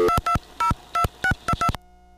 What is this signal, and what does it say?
DTMF, heard as telephone touch tones, tones of about 100 ms. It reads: 2202622